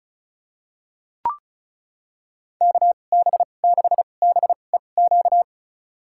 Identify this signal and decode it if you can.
Morse code, heard as a keyed tone, about 36 wpm, spaced as for 25 wpm, 700 Hz. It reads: KB6BEQ
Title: USA Call Signs - 6 Characters - 2x Character Spacing - 35wpm